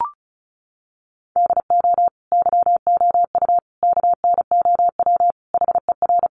Morse code, heard as a keyed tone, about 34 wpm, 700 Hz. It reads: DO YOU KNOW HER
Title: Sentences from Top 1400 Words 35wpm